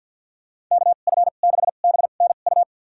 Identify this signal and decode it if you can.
Morse code, heard as a keyed tone, about 50 wpm, spaced as for 36 wpm, 700 Hz. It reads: KF6BNU
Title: USA Call Signs - 2x Character Spacing - 50wpm